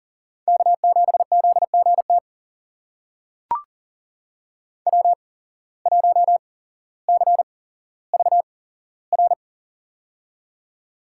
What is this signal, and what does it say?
Morse code, heard as a keyed tone, about 41 wpm, spaced as for 17 wpm, 700 Hz. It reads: K7ZGT W1CVR
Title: USA Call Signs - 5 Characters - 8x Character Spacing - 40wpm